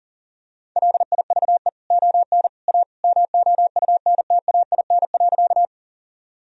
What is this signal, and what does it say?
Morse code, heard as a keyed tone, about 40 wpm, 700 Hz. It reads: LIVE ON A MOUNTAIN.
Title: Sentences from Top 1000 Words - Review Part 1 - 40wpm